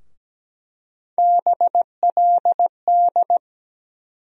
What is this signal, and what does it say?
Morse code, heard as a keyed tone, about 17 wpm, 700 Hz. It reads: BLD
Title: USA Call Sign Postfixes - 3 Letters 17wpm